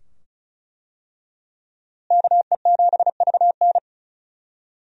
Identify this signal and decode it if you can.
Morse code, heard as a keyed tone, about 34 wpm, 700 Hz. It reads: KE7VN
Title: USA Call Signs - 5 Characters - Repeated 2x - 35wpm